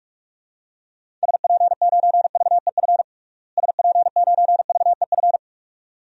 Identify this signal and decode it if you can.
Morse code, heard as a keyed tone, about 47 wpm, 700 Hz. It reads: SP9VEF SP9VEF
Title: DXCC-100 Call Signs - 6 Characters - Repeated 2x - 45wpm